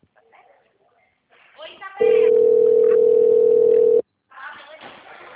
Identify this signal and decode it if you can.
Ringback tone (North America), 440 + 480 Hz, with speech.
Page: Piracicaba